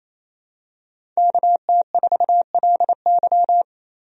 Morse code, heard as a keyed tone, about 28 wpm, 700 Hz. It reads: KT4LY
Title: USA Call Signs - 5 Characters - Repeated 2x - 28wpm